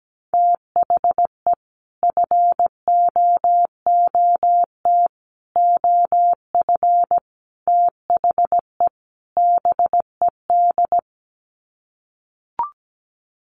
Morse code, heard as a keyed tone, about 17 wpm, 700 Hz. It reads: THE FOOT OF THE BED
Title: Sets of 5 Words Encore2 17wpm